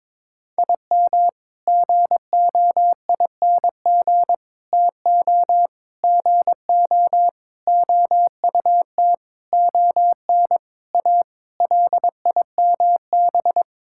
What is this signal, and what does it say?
Morse code, heard as a keyed tone, about 21 wpm, 700 Hz. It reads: IM GOING TO GO OUT ON A LIMB
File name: practice / 22wpm / Sentences with 8-Word Phrases - 22wpm